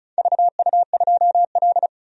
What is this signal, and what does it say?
Morse code, heard as a keyed tone, about 36 wpm, 700 Hz. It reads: VU2L
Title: DXCC-100 Call Signs - 4 Characters - 35wpm